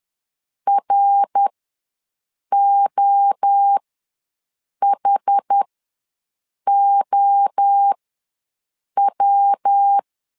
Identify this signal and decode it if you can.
Morse code, heard as a keyed tone, about 10 wpm, spaced as for 6 wpm, 800 Hz. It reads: ROHOW
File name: LEZIONI / 3 MWRO / 3 MWRO ws 10wpm 2310